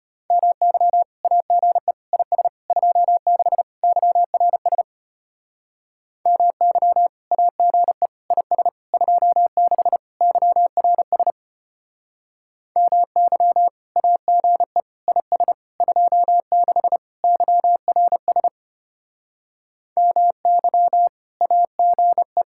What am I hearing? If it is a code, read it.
Morse code, heard as a keyed tone, about 32 wpm, 700 Hz. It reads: MY AGE IS 26 YRS MY AGE IS 26 YRS MY AGE IS 26 YRS MY AGE